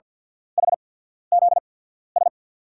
Morse code, heard as a keyed tone, about 47 wpm, spaced as for 14 wpm, 700 Hz. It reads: HZS
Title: USA Call Signs - 5 Characters - 8x Character Spacing - 50wpm